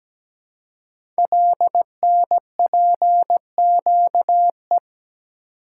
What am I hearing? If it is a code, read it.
Morse code, heard as a keyed tone, about 17 wpm, 700 Hz. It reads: LNPQE
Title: Code Groups - 5 Letters - 17wpm